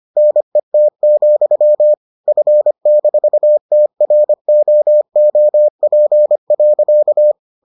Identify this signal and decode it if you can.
Morse code, heard as a keyed tone, about 25 wpm, 600 Hz. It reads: NET, F-TROOP.